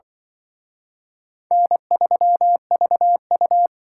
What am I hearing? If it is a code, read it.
Morse code, heard as a keyed tone, about 24 wpm, 700 Hz. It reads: N3VU
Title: Speed Racing - USA 1x2 Call Signs 20wpm